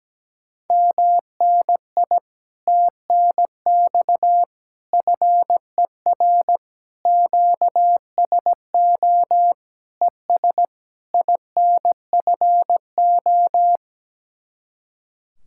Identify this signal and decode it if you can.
Morse code, heard as a keyed tone, about 17 wpm, 700 Hz. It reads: MNI TNX FER QSO ES INFO
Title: Speed-Racing Common QSO Phrases Encore - 17wpm